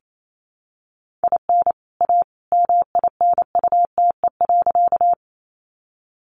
Morse code, heard as a keyed tone, about 28 wpm, 700 Hz. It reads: IN A MINUTE.
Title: Sentences from Top 500 Words 28wpm